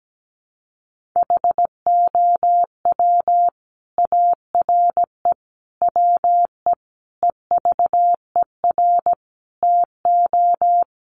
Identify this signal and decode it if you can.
Morse code, heard as a keyed tone, about 17 wpm, 700 Hz. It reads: HOW ARE WE EVER TO
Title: Sentences from Top 1500 Words 17wpm